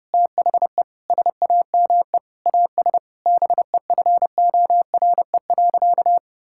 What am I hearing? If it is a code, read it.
Morse code, heard as a keyed tone, about 30 wpm, 700 Hz. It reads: THE SAME AS BEFORE.
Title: Sentences from Top 1000 Words - Review Part 6 - 30wpm